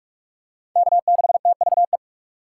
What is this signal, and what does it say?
Morse code, heard as a keyed tone, about 41 wpm, 700 Hz. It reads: K6TVE
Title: USA Call Signs - 5 Characters - 3x Character Spacing - 45wpm